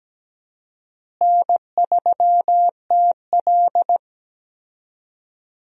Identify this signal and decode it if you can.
Morse code, heard as a keyed tone, about 17 wpm, 700 Hz. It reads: N3TL
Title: USA Call Signs - 4 Characters - Repeated 2x - 17wpm